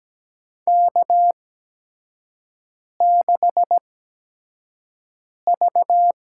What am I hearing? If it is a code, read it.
Morse code, heard as a keyed tone, about 17 wpm, spaced as for 5 wpm, 700 Hz. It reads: K6V
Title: USA Call Signs - 8x Character Spacing - 17wpm